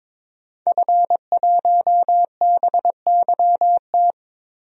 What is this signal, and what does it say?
Morse code, heard as a keyed tone, about 22 wpm, 700 Hz. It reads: F1BYT
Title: DXCC-100 Call Signs - 5 Characters - Repeated 3x - 22wpm